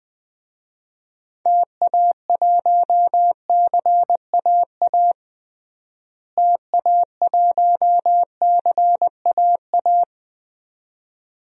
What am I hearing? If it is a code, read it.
Morse code, heard as a keyed tone, about 20 wpm, 700 Hz. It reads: TA1CAA TA1CAA
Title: DXCC-100 Call Signs - 6 Characters - Repeated 2x - 20wpm